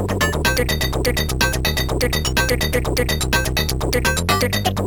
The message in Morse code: none